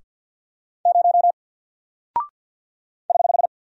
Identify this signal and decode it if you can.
Morse code, heard as a keyed tone, about 50 wpm, 700 Hz. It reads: 0 <HH>
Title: Single Character (Letter:Number:Punctuation:Prosign) 50wpm